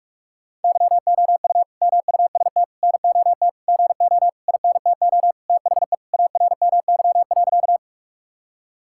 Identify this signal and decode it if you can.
Morse code, heard as a keyed tone, about 46 wpm, 700 Hz. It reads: YOU MUST NOT GO INTO THE ARMY.
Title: Sentences from Top 1000 Words - Review Part 5 - 45wpm